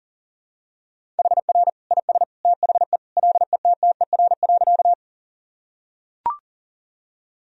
Morse code, heard as a keyed tone, about 40 wpm, 700 Hz. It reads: HR IS THE LETTER.
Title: Sentences from Top 300 Words and Abbreviations 40wpm